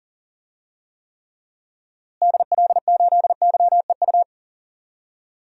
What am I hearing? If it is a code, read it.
Morse code, heard as a keyed tone, about 40 wpm, 700 Hz. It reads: DL8YEU
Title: DXCC-100 Call Signs - 6 Characters - Repeated 2x - 40wpm